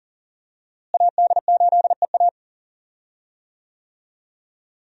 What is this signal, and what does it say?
Morse code, heard as a keyed tone, about 40 wpm, 700 Hz. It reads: AD8EA
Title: USA Call Signs - 5 Characters - Repeated 3x - 40wpm